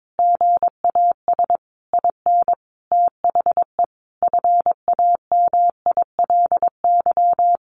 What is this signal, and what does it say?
Morse code, heard as a keyed tone, about 22 wpm, 700 Hz. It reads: GAS IN THE FAMILY
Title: Sentences from Top 1100 Words 22wpm